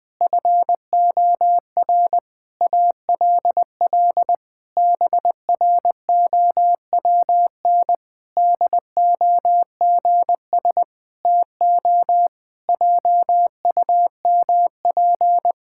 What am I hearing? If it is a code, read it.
Morse code, heard as a keyed tone, about 19 wpm, 700 Hz. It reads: FOR ALL BROWN DOGS TO JUMP
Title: Pangram Challenge v2 - 20wpm